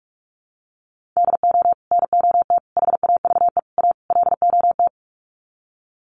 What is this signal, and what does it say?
Morse code, heard as a keyed tone, about 44 wpm, 700 Hz. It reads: DO NOT HAVE A LOT
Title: Sets of 5 Words Encore 45wpm